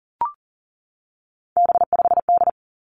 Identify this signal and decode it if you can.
Morse code, heard as a keyed tone, about 42 wpm, 690 Hz. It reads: B5D
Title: Three Challenging Letter-Number Combinations 40wpm